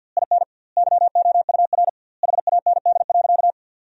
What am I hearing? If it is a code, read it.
Morse code, heard as a keyed tone, about 50 wpm, 700 Hz. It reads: IN YOUR HAND.
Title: 2x Word-Spacing - Sentences from Top 400 Words - 50wpm